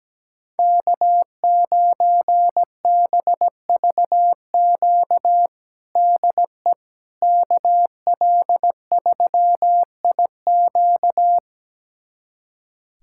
Morse code, heard as a keyed tone, about 17 wpm, 700 Hz. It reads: K9BVQ DE KL3IQ
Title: USA Callsign DE Callsign - 5 Characters - 17wpm